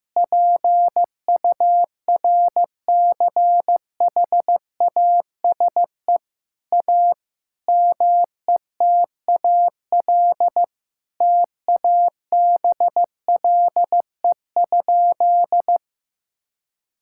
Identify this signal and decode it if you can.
Morse code, heard as a keyed tone, about 15 wpm, 700 Hz. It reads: PURCHASE A METAL TABLE?
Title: Sentences from Top 1000 Words - Review Part 6 - 15wpm